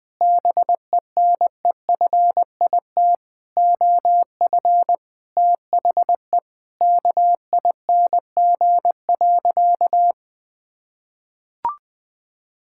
Morse code, heard as a keyed tone, about 19 wpm, 700 Hz. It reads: BENEFIT OF THE KING.
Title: Sentences from Top 1000 Words - Review Part 6 - 20wpm